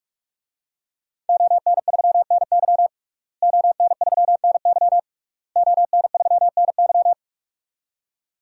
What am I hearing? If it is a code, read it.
Morse code, heard as a keyed tone, about 46 wpm, 700 Hz. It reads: ON3NY ON3NY ON3NY
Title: DXCC-100 Call Signs - 5 Characters - Repeated 3x - 45wpm